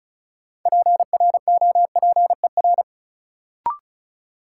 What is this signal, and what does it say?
Morse code, heard as a keyed tone, about 34 wpm, 700 Hz. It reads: PROPER
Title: Sentences from Top 800 Words - New Word Drill - 35wpm